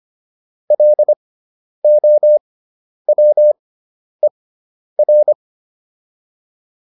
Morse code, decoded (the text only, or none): LOWER